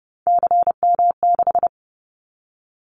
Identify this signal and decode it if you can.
Morse code, heard as a keyed tone, about 30 wpm, 700 Hz. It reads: CM6
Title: Callsign Prefixes for Top 100 DXCC - 30wpm